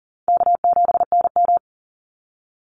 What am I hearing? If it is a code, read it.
Morse code, heard as a keyed tone, about 40 wpm, 700 Hz. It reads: K7NM
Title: Easy Call Signs Repeated 3x - 40wpm